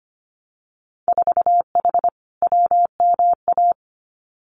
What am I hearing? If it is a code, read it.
Morse code, heard as a keyed tone, about 25 wpm, 700 Hz. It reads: 4H WMA
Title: Winter Field Day Exchanges Encore 25wpm